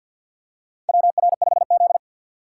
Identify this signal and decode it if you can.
Morse code, heard as a keyed tone, about 52 wpm, 700 Hz. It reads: WR5Z
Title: USA Call Signs - 4 Characters - 2x Character Spacing - 50wpm